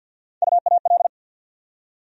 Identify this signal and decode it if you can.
Morse code, heard as a keyed tone, about 50 wpm, 700 Hz. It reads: UAL
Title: USA Call Sign Postfixes - 3 Letters 50wpm